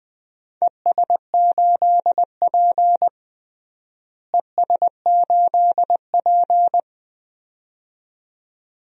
Morse code, heard as a keyed tone, about 20 wpm, 700 Hz. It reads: ES8P ES8P
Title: DXCC-100 Call Signs - 4 Characters - Repeated 2x - 20wpm